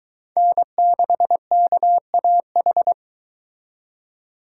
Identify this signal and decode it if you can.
Morse code, heard as a keyed tone, about 23 wpm, 700 Hz. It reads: N6KAH